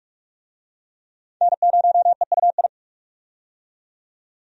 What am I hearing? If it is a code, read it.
Morse code, heard as a keyed tone, about 41 wpm, 700 Hz. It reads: N0EUI
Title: USA Call Signs - 5 Characters - 45wpm